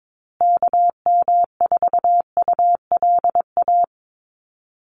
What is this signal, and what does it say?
Morse code, heard as a keyed tone, about 22 wpm, 700 Hz. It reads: KM4ULA
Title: USA Call Signs Repeated 2x 22wpm